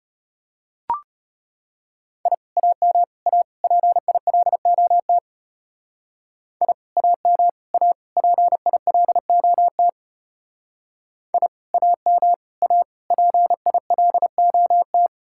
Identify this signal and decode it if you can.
Morse code, heard as a keyed tone, about 32 wpm, 700 Hz. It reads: I AM A PILOT I AM A PILOT I AM A PILOT